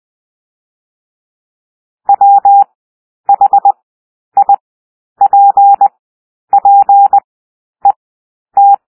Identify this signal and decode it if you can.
Morse code, heard as a keyed tone, about 20 wpm, 800 Hz. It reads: W H I P P E T
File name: cwmp3 / W1 SOUND WORDS 20 wpm